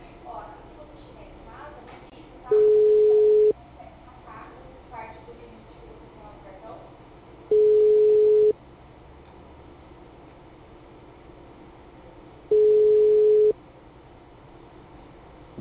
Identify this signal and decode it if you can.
Ringback tone (Europe, 425 Hz), with speech.